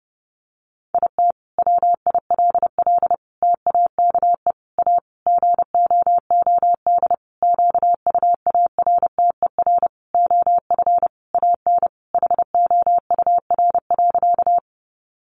Morse code, heard as a keyed tone, about 30 wpm, 700 Hz. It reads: IT WILL TAKE A GOOD QUARTER OF AN HOUR.
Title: Sentences from Top 1300 Words 30wpm